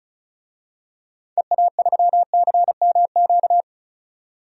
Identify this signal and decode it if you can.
Morse code, heard as a keyed tone, about 34 wpm, 700 Hz. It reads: EA3CMQ
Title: DXCC-100 Call Signs - 6 Characters - 35wpm